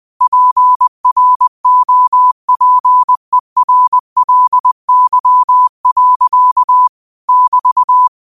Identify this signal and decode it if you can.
Morse code, heard as a keyed tone, about 20 wpm, 1000 Hz. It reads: PROPERLY. <BT>
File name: qotdfiles / 20wpm